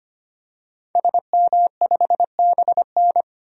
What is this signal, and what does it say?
Morse code, heard as a keyed tone, about 25 wpm, 700 Hz. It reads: SM5BN
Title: DXCC-100 Call Signs - 5 Characters - 25wpm